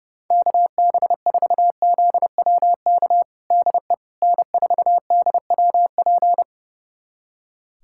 Morse code, heard as a keyed tone, about 32 wpm, 700 Hz. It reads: KB4ZWK DE N4DWP
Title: USA Callsign DE Callsign 30wpm